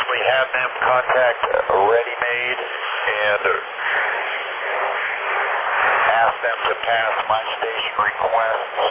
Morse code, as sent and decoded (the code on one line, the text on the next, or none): none